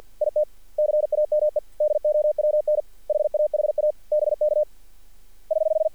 Morse code, heard as a keyed tone, about 53 wpm, 590 Hz. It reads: IT CAME DOWN HAHA BK <HH>
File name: archived / CRSnet